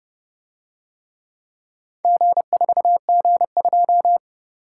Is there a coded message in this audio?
Yes: Morse code from 2.0 to 4.2 s, heard as a keyed tone, about 30 wpm, 700 Hz. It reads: G4G2